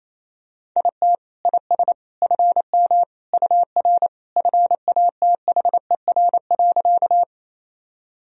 Morse code, heard as a keyed tone, about 28 wpm, 700 Hz. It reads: IT IS FM UR FATHER.
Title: Sentences from Top 300 Words and Abbreviations 28wpm